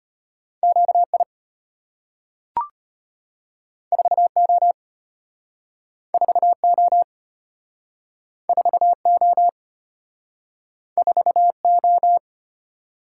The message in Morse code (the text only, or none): QI 4O 4O 4O 4O